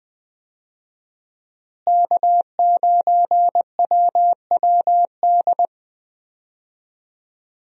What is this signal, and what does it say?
Morse code, heard as a keyed tone, about 20 wpm, 700 Hz. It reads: K9WWD
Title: USA Call Signs - 5 Characters - Repeated 3x - 20wpm